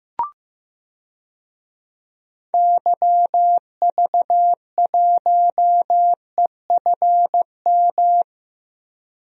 Morse code, heard as a keyed tone, about 15 wpm, 700 Hz. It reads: YV1EFM
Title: DXCC-100 Call Signs - 6 Characters - Repeated 3x - 15wpm